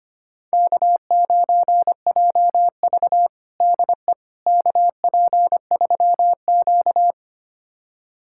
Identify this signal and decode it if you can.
Morse code, heard as a keyed tone, about 25 wpm, 700 Hz. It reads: K9JV DE KP3Q
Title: USA Callsign DE Callsign - 4 Characters - 25wpm